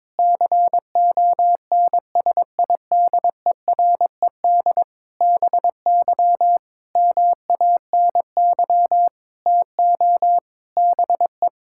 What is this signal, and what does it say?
Morse code, heard as a keyed tone, about 22 wpm, 700 Hz. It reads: CONSIDERED BY MANY TO BE
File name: practice / 22wpm / Sets of 7 Words - 22wpm